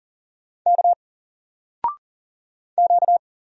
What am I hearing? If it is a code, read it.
Morse code, heard as a keyed tone, about 39 wpm, 700 Hz. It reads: K Q
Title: Single Letter-Number 40wpm